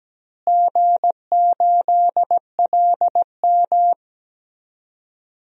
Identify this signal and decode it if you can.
Morse code, heard as a keyed tone, about 17 wpm, 700 Hz. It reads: G8LM
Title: DXCC-100 Call Signs - 4 Characters - Repeated 3x - 17wpm